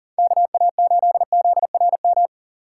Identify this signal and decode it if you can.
Morse code, heard as a keyed tone, about 50 wpm, 700 Hz. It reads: KA8ZRM
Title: Speed Racing - USA 2x3 Call Signs 40wpm